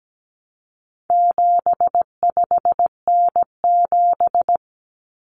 Morse code, heard as a keyed tone, about 17 wpm, 700 Hz. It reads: 75N7
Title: Common Cut Numbers - 4-Digit 17wpm